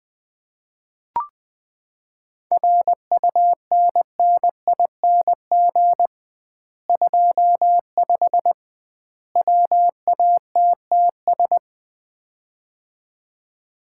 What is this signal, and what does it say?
Morse code, heard as a keyed tone, about 20 wpm, 700 Hz. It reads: RUNNING 25 WATTS
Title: Common QSO Phrases - 2x Word-Spacing - 20wpm